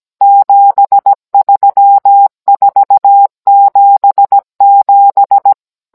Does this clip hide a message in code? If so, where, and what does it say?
Yes: Morse code from 0.2 to 5.5 s, heard as a keyed tone, about 17 wpm, 800 Hz. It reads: 73477